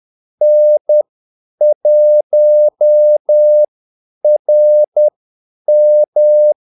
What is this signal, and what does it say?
Morse code, heard as a keyed tone, about 10 wpm, spaced as for 8 wpm, 600 Hz. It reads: N1RM
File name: Callsigns-10-8wpm